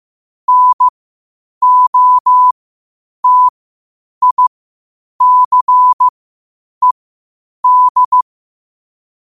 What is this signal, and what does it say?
Morse code, heard as a keyed tone, about 15 wpm, spaced as for 8 wpm, 1000 Hz. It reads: NOTICED
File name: qotdfiles / 8wpm